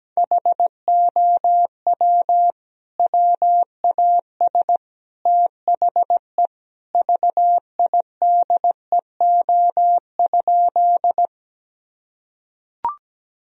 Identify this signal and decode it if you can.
Morse code, heard as a keyed tone, about 17 wpm, 700 Hz. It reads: HOW WAS THE VIDEO?
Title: Sentences from Top 1000 Words - Review Part 4 - 17wpm